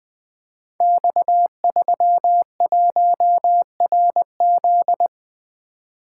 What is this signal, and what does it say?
Morse code, heard as a keyed tone, about 20 wpm, 700 Hz. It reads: X31RZ
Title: Code Groups - 5 Letter:Numbers - 20wpm